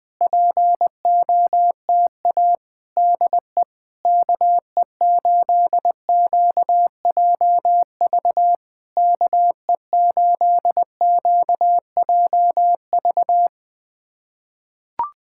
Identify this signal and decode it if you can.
Morse code, heard as a keyed tone, about 20 wpm, 700 Hz. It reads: POTA DE KE8QJV KE8QJV
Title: POTA QSO Practice - 20wpm